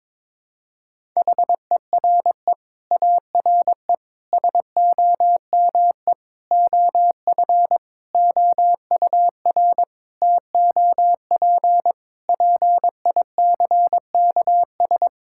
Morse code, heard as a keyed tone, about 22 wpm, 700 Hz. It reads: HERE ARE SOME OF OUR TOP PICKS
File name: Sets of 9 Words - 22wpm